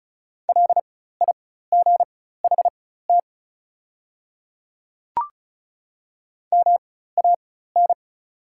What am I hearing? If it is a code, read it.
Morse code, heard as a keyed tone, about 36 wpm, spaced as for 16 wpm, 700 Hz. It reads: LIGHT MAN
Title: 4x Character Spacing - Top 200 Words - 35wpm